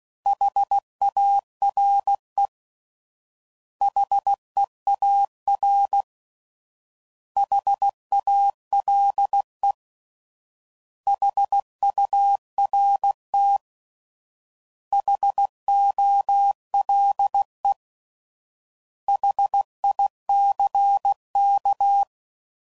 Morse code, heard as a keyed tone, about 16 wpm, 800 Hz. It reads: HARE HEAR HALE HURT HOLE HICK